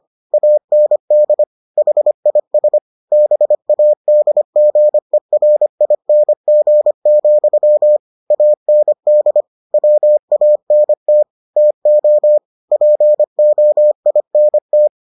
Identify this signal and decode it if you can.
Morse code, heard as a keyed tone, about 25 wpm, 600 Hz. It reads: AND HIS BADGERING, AND WANT TO POINT